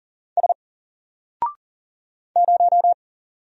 Morse code, heard as a keyed tone, about 40 wpm, 700 Hz. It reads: S 0
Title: Single Character (Letter:Number:Punctuation:Prosign) 40wpm